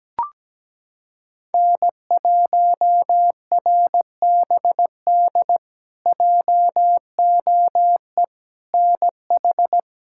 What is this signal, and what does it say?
Morse code, heard as a keyed tone, about 17 wpm, 700 Hz. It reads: N1RBD JOE NH
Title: K1USN SST QSO Practice - 17wpm